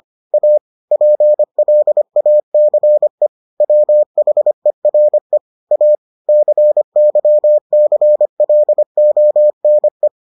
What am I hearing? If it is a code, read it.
Morse code, heard as a keyed tone, about 25 wpm, 600 Hz. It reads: A PLACE WHERE A CYCLONE